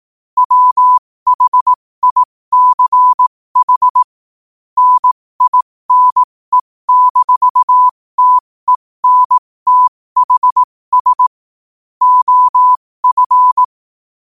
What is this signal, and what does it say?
Morse code, heard as a keyed tone, about 18 wpm, spaced as for 15 wpm, 1000 Hz. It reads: WHICH NINE-TENTHS OF
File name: qotdfiles / 15wpm